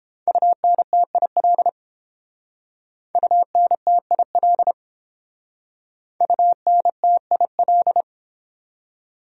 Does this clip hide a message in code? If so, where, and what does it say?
Yes: Morse code from 0.3 to 8.0 s, heard as a keyed tone, about 29 wpm, 700 Hz. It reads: UNTIL UNTIL UNTIL